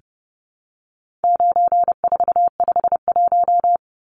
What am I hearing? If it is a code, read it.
Morse code, heard as a keyed tone, about 30 wpm, 700 Hz. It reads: 9451
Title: Numbers - 4-Digit 30wpm